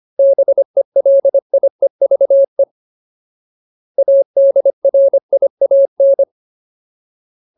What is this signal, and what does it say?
Morse code, heard as a keyed tone, about 25 wpm, 550 Hz. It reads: BELIEVE ADRIAN
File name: Ultimate Word List / 25WPM Ultimate